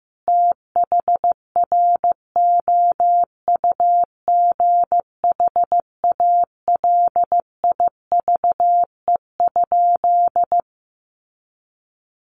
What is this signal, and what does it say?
Morse code, heard as a keyed tone, about 14 wpm, 700 Hz. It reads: THROUGH ALIVE?
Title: A Sound Of Thunder 15wpm